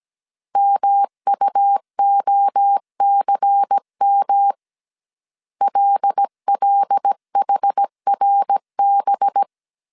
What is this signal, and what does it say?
Morse code, heard as a keyed tone, about 17 wpm, 800 Hz. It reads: MUOCM LLHRB
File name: 16wpm